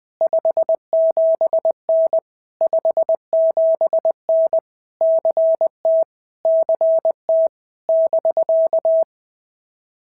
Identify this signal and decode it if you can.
Morse code, heard as a keyed tone, about 19 wpm, 650 Hz. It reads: 57N 57N CT CT <BK>
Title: POTA QSO Practice - 20wpm